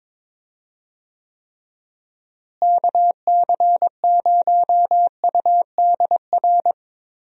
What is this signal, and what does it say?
Morse code, heard as a keyed tone, about 23 wpm, 700 Hz. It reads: KC0UDR
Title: USA Call Signs - 6 Characters - Repeated 2x - 22wpm